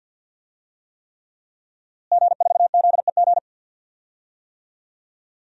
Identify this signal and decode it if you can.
Morse code, heard as a keyed tone, about 48 wpm, 700 Hz. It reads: G4ZEG
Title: DXCC-100 Call Signs - 5 Characters - Repeated 3x - 50wpm